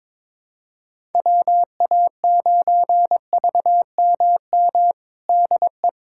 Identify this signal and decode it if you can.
Morse code, heard as a keyed tone, about 22 wpm, 700 Hz. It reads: WA9VMM DE
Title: USA Callsign DE Callsign - 6 Characters - 22wpm